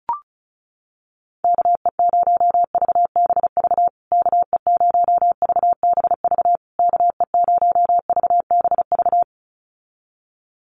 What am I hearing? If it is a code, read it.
Morse code, heard as a keyed tone, about 34 wpm, 700 Hz. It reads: KE0VBV KE0VBV KE0VBV
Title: USA Call Signs Repeated 3x 35wpm